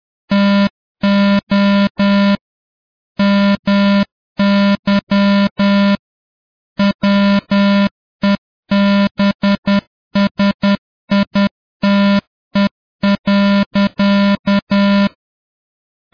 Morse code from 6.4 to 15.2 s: WEBSITE.